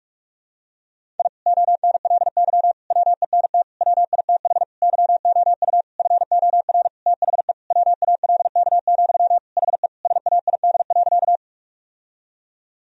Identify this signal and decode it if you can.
Morse code, heard as a keyed tone, about 46 wpm, 700 Hz. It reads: I ONLY WENT WITH YOU FOR THE WALK, HE SAID.
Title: Sentences from Top 1000 Words - Review Part 1 - 45wpm